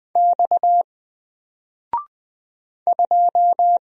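Morse code, heard as a keyed tone, about 20 wpm, 700 Hz. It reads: X 2
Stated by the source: Single letter and number Morse drill at 20 wpm, from N4LCD.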